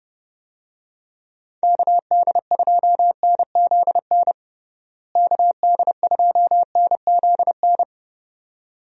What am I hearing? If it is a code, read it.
Morse code, heard as a keyed tone, about 30 wpm, 700 Hz. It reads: KD2NZN KD2NZN
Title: USA Call Signs - 6 Characters - Repeated 2x - 30wpm